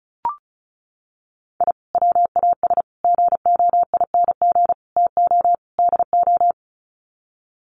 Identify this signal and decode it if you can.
Morse code, heard as a keyed tone, about 34 wpm, 700 Hz. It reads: I WAS GOING TO DO
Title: Sets of 5 Words Encore2 35wpm